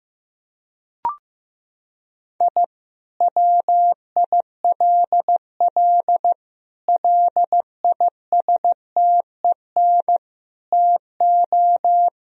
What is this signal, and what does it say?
Morse code, heard as a keyed tone, about 14 wpm, 700 Hz. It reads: I WILL LISTEN TO
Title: Sentences from Top 1000 Words - Review Part 3 - 15wpm